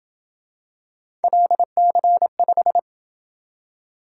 Morse code, heard as a keyed tone, about 27 wpm, 700 Hz. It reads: LC5